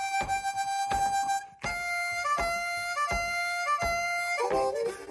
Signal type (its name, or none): computer data on cassette